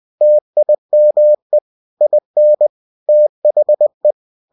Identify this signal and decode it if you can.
Morse code, heard as a keyed tone, about 20 wpm, 600 Hz. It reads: TIME IN THE
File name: cw-books / winnie-the-pooh / winnie-20wpm-page-0030